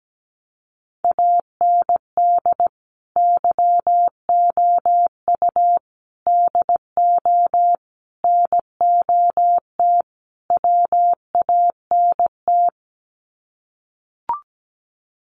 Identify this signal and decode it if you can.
Morse code, heard as a keyed tone, about 17 wpm, 700 Hz. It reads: AND YOU DO NOT WANT
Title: Sets of 5 Words Encore2 17wpm